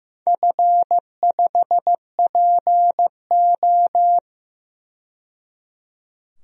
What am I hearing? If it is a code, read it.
Morse code, heard as a keyed tone, about 15 wpm, 700 Hz. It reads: F5PO
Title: DXCC-100 Call Signs - 4 Characters - 15wpm